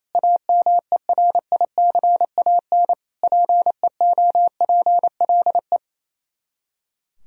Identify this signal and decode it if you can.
Morse code, heard as a keyed tone, about 36 wpm, 700 Hz. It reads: AMERICAN PEOPLE